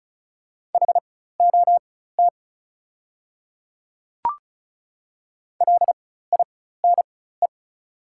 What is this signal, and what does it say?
Morse code, heard as a keyed tone, about 34 wpm, spaced as for 16 wpm, 700 Hz. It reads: HOT LINE
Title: 4x Character Spacing - Top 200 Words - 35wpm